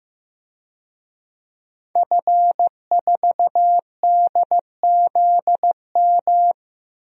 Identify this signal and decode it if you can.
Morse code, heard as a keyed tone, about 15 wpm, 700 Hz. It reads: F4DZM
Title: DXCC-100 Call Signs - 5 Characters - 15wpm